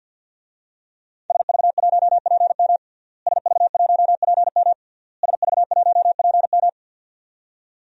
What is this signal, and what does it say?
Morse code, heard as a keyed tone, about 52 wpm, 700 Hz. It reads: SV1PM SV1PM SV1PM
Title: DXCC-100 Call Signs - 5 Characters - Repeated 3x - 50wpm